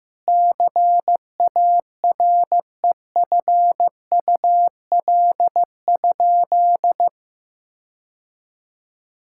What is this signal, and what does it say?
Morse code, heard as a keyed tone, about 15 wpm, 700 Hz. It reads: CAREFUL?
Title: Sentences from Top 1600 Words 15wpm